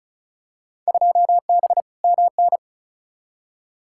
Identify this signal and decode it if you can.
Morse code, heard as a keyed tone, about 36 wpm, 700 Hz. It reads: 2B MN
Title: Field Day Exchanges - 35wpm